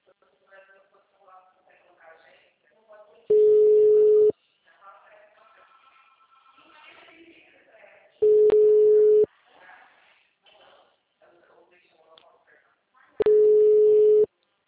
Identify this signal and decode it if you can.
Ringback tone (Europe, 425 Hz), with speech.
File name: out- -2031- - - .32105